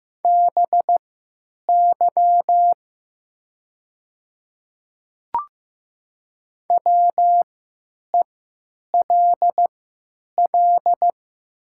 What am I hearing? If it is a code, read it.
Morse code, heard as a keyed tone, about 15 wpm, spaced as for 9 wpm, 700 Hz. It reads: BY WELL
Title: 3x Character Spacing - Top 300 Words - 15wpm